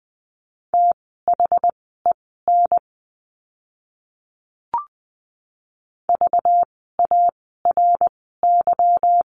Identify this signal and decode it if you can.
Morse code, heard as a keyed tone, about 20 wpm, spaced as for 14 wpm, 700 Hz. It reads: THEN VARY
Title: 192 Lesson 31f - Letter v - Words 20wpm